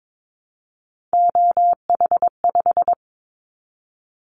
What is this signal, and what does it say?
Morse code, heard as a keyed tone, about 22 wpm, 700 Hz. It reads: OH5
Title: Callsign Prefixes for Top 50 DXCC - 22wpm